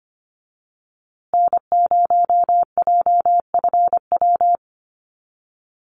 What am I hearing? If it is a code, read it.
Morse code, heard as a keyed tone, about 25 wpm, 700 Hz. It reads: N0JFW
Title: USA Call Signs Repeated 2x 25wpm